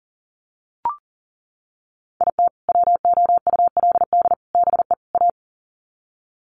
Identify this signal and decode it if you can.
Morse code, heard as a keyed tone, about 40 wpm, 700 Hz. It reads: IT WOULD BE A
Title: Sets of 4 Words 40wpm